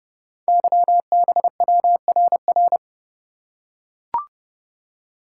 Morse code, heard as a keyed tone, about 30 wpm, 700 Hz. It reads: YBWRR
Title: Code Groups - 5 Letters - 30wpm